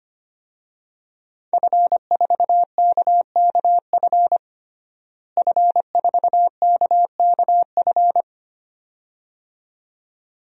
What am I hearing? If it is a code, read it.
Morse code, heard as a keyed tone, about 25 wpm, 700 Hz. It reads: F4KKF F4KKF
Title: DXCC-100 Call Signs - 5 Characters - Repeated 2x - 25wpm